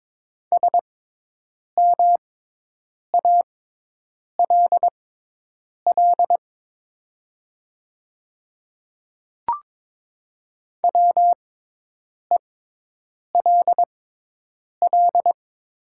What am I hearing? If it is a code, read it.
Morse code, heard as a keyed tone, about 22 wpm, spaced as for 8 wpm, 700 Hz. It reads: SMALL WELL
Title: 6x Character Spacing - Top 300 Words - 22wpm